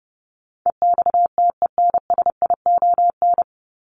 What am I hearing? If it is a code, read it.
Morse code, heard as a keyed tone, about 30 wpm, 700 Hz. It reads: EXTENSION
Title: Top 1501-1750 Words 30wpm